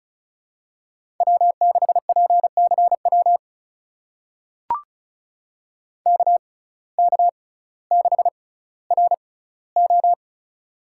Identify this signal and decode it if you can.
Morse code, heard as a keyed tone, about 34 wpm, spaced as for 18 wpm, 700 Hz. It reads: W6PCW KK6RO
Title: USA Call Signs - 5 Characters - 6x Character Spacing - 35wpm